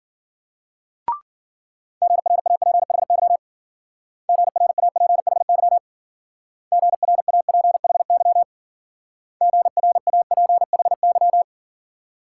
Morse code, heard as a keyed tone, about 50 wpm, 700 Hz. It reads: GRAPHY GRAPHY GRAPHY GRAPHY
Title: Speed-Racing Multi-letter Phonemes - 40wpm